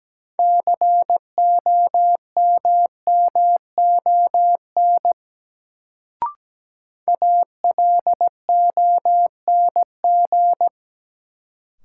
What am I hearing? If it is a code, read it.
Morse code, heard as a keyed tone, about 17 wpm, 700 Hz. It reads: COMMON ALONG
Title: Sentences from Top 700 Words - New Word Drill - 17wpm